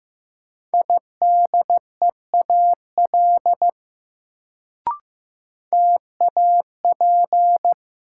Morse code, heard as a keyed tone, about 15 wpm, 700 Hz. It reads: IDEAL TAP
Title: Sentences from Top 1400 Words - New Word Drill - 15wpm